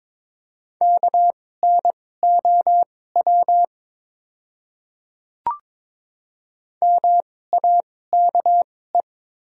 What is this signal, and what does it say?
Morse code, heard as a keyed tone, about 22 wpm, spaced as for 16 wpm, 690 Hz. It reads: KNOW MAKE
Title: 2x Character Spacing - Top 100 Words - 22wpm